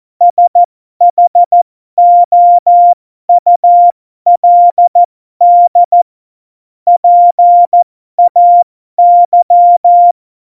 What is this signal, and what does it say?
Morse code, heard as a keyed tone, about 14 wpm, spaced as for 12 wpm, 700 Hz. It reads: SHOULD PAY
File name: Morsekurs / Witze